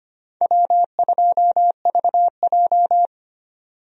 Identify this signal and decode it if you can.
Morse code, heard as a keyed tone, about 25 wpm, 700 Hz. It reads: W2VJ
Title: USA Call Signs - 4 Characters - 25wpm